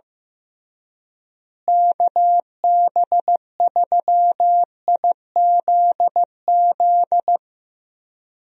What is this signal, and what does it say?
Morse code, heard as a keyed tone, about 15 wpm, 700 Hz. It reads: KB3IZZ
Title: USA Call Signs - 6 Characters - Repeated 2x - 15wpm